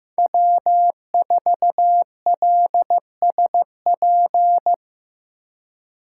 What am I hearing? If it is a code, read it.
Morse code, heard as a keyed tone, about 15 wpm, 700 Hz. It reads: W4LSP
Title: USA Call Signs - 2x Character Spacing - 15wpm